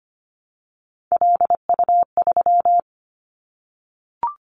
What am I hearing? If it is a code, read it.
Morse code, heard as a keyed tone, about 25 wpm, 690 Hz. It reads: LU3
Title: Callsign Prefixes for Top 50 DXCC - 25wpm